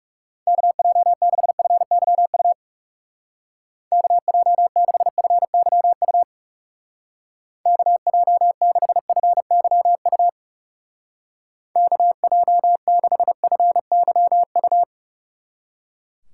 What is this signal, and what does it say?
Morse code, heard as a keyed tone, about 37 wpm, 700 Hz. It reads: KJ6FYU KJ6FYU KJ6FYU KJ6FYU
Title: Speed Racing - USA 2x3 Call Signs 30wpm